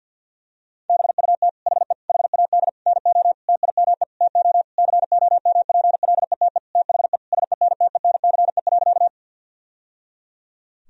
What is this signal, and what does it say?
Morse code, heard as a keyed tone, about 50 wpm, 700 Hz. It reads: BUT HE HAD NO TIME TO COMPLETE THE SENTENCE.
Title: Sentences from Top 1000 Words - Review Part 1 - 50wpm